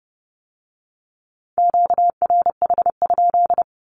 Morse code, heard as a keyed tone, about 30 wpm, 690 Hz. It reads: QRH?
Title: Q-Codes with Spoken Meaning - 30wpm